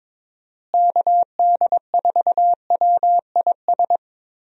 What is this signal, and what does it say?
Morse code, heard as a keyed tone, about 22 wpm, 700 Hz. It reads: KD4WIS
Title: USA Call Signs - 6 Characters - Repeated 2x - 22wpm